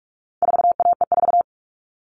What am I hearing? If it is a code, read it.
Morse code, heard as a keyed tone, about 44 wpm, 700 Hz. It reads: 4AE4
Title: All Cut Numbers - 4-Digit 45wpm